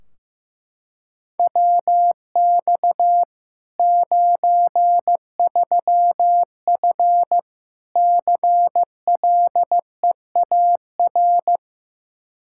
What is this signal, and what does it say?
Morse code, heard as a keyed tone, about 15 wpm, 700 Hz. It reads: WX 93F CLEAR
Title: Practice QSO - Exchange 2 - Info - 15wpm